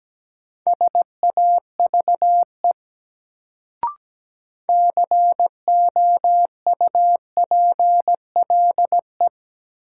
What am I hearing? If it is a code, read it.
Morse code, heard as a keyed tone, about 17 wpm, 700 Hz. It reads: SAVE COUPLE
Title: Sentences from Top 1100 Words - New Word Drill Alt - 17wpm